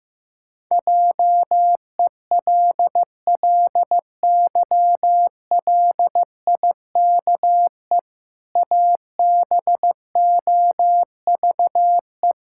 Morse code, heard as a keyed tone, about 15 wpm, 700 Hz. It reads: JELLYLIKE ABOVE
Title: Pangram Challenge v2 - 15wpm